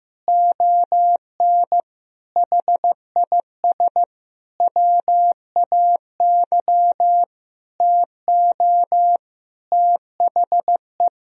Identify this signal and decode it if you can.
Morse code, heard as a keyed tone, about 15 wpm, 700 Hz. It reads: ON HIS WAY TO THE
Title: Sentences from Top 1000 Words - Review Part 2 - 15wpm